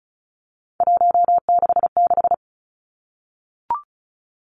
Morse code, heard as a keyed tone, about 36 wpm, 700 Hz. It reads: 166
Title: Three Challenging Letter-Number Combinations 35wpm